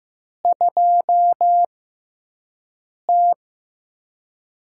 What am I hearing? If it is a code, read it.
Morse code, heard as a keyed tone, about 15 wpm, spaced as for 5 wpm, 700 Hz. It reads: 2T
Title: USA Call Signs - 5 Characters - 6x Character Spacing - 15wpm